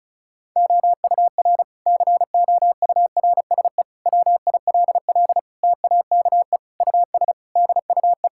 Morse code, heard as a keyed tone, about 36 wpm, 700 Hz. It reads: OUR COURSE WILL TAKE US DUE
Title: Sentences from Top 1000 Words - Review Part 6 - 35wpm